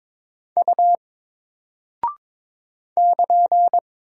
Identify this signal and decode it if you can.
Morse code, heard as a keyed tone, about 22 wpm, 700 Hz. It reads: U <KN>
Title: Single Character (Letter:Number:Punctuation:Prosign) Alt 22wpm